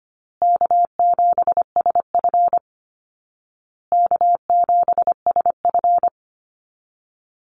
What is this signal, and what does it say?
Morse code, heard as a keyed tone, about 25 wpm, 700 Hz. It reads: K7SF K7SF
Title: Easy Call Signs Repeated 3x - 25wpm